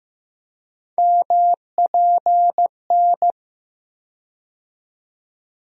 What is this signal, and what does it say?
Morse code, heard as a keyed tone, about 15 wpm, 700 Hz. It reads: MPN
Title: USA Call Sign Postfixes - 3 Letters 15wpm